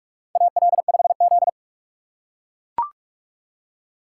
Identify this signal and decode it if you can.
Morse code, heard as a keyed tone, about 42 wpm, 700 Hz. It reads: AL5Z
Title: USA Call Signs - 4x Character Spacing - 45wpm